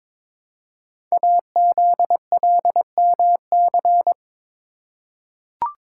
Morse code, heard as a keyed tone, about 22 wpm, 700 Hz. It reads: AZLMC
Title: Code Groups - 5 Letters - 22wpm